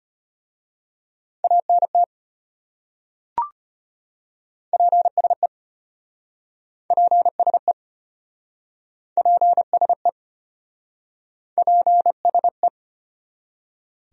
Morse code, heard as a keyed tone, about 34 wpm, 700 Hz. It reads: ANT PSE PSE PSE PSE